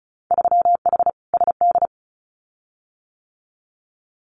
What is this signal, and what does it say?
Morse code, heard as a keyed tone, about 34 wpm, 700 Hz. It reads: 3H SD
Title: Winter Field Day Exchanges Encore 35wpm